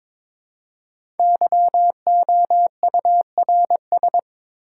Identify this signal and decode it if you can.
Morse code, heard as a keyed tone, about 22 wpm, 700 Hz. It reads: YOURS
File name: 5 Letter Words - First 3 Letters Given - 22wpm